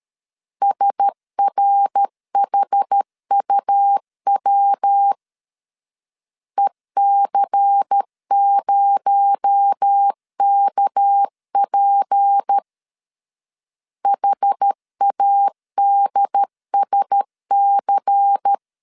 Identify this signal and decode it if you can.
Morse code, heard as a keyed tone, about 13 wpm, 800 Hz. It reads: SRHUW EC0KP HADSC
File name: CW / DET / 1 6 1500C S B 12wpm 3431